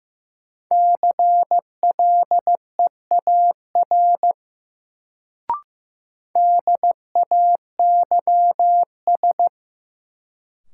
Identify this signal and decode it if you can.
Morse code, heard as a keyed tone, about 15 wpm, 700 Hz. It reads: CLEAR DAYS
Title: Sets of 5 Words Encore - New Word Drill - 15wpm